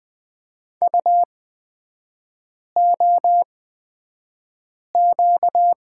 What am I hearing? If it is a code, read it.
Morse code, heard as a keyed tone, about 20 wpm, spaced as for 5 wpm, 700 Hz. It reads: UOQ
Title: 236 Lesson 38e - Letter q - Three Characters 20wpm